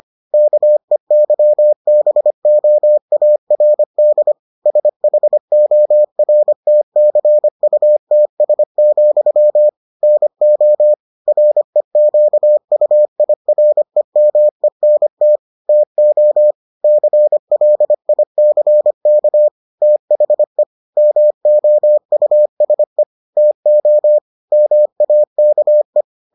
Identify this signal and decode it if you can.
Morse code, heard as a keyed tone, about 25 wpm, 600 Hz. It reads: KEYBOARD SHORTCUTS, NO REQUIREMENT TO CLICK THE MOUSE TO MAKE